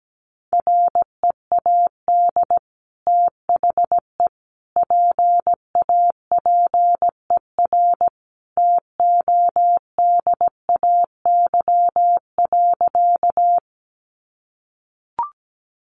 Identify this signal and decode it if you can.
Morse code, heard as a keyed tone, about 17 wpm, 700 Hz. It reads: READ THE PAPER TODAY.
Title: Sentences from Top 600 Words 17wpm